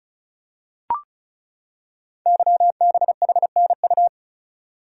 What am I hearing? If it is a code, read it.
Morse code, heard as a keyed tone, about 36 wpm, 700 Hz. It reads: YBHNU
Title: Code Groups - 5 Letter:Numbers - 35wpm